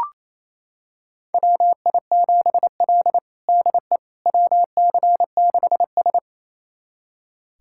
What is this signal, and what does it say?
Morse code, heard as a keyed tone, about 27 wpm, 700 Hz. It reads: WI7L DE WC6S
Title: USA Callsign DE Callsign - 4 Characters - 28wpm